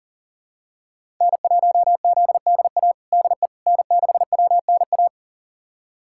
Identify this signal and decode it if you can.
Morse code, heard as a keyed tone, about 40 wpm, 700 Hz. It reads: N1ZDA DE N6WNA
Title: USA Callsign DE Callsign - 5 Characters - 40wpm